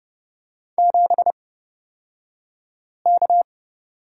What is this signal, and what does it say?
Morse code, heard as a keyed tone, about 31 wpm, spaced as for 5 wpm, 700 Hz. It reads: 7K
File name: Lesson 36d - BT - Two Characters 30wpm